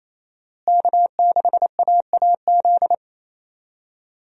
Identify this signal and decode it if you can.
Morse code, heard as a keyed tone, about 28 wpm, 700 Hz. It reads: K6AAZ
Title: USA Call Signs - 5 Characters - Repeated 3x - 28wpm